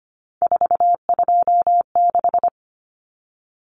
Morse code, heard as a keyed tone, about 25 wpm, 700 Hz. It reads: 426
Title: Common Cut Numbers - 3-Digit 25wpm